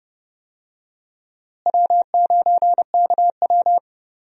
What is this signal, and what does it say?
Morse code, heard as a keyed tone, about 30 wpm, 700 Hz. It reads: W9KW
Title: USA Call Signs - 4 Characters - Repeated 3x - 30wpm